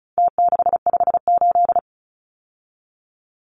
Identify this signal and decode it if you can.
Morse code, heard as a keyed tone, about 34 wpm, 700 Hz. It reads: T658
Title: Numbers - Contest Style - 4-Digit 35wpm